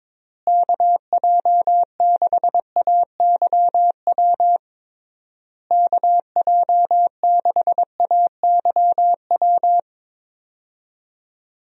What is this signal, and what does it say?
Morse code, heard as a keyed tone, about 22 wpm, 700 Hz. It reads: KJ6AYW KJ6AYW
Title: USA Call Signs - 6 Characters - Repeated 3x - 22wpm